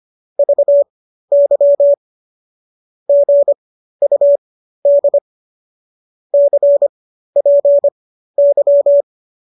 Morse code, heard as a keyed tone, about 25 wpm, spaced as for 13 wpm, 570 Hz. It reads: VY GUD CPY